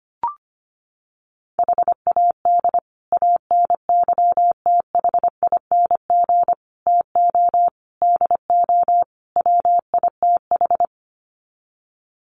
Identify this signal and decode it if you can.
Morse code, heard as a keyed tone, about 25 wpm, 700 Hz. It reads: HAD ANYTHING TO DO WITH
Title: Sets of 5 Words Encore 25wpm